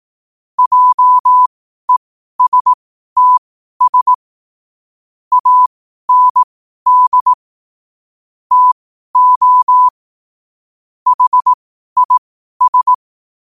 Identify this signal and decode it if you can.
Morse code, heard as a keyed tone, about 18 wpm, spaced as for 12 wpm, 1000 Hz. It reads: JESTS AND TO HIS
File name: qotdfiles / 12wpm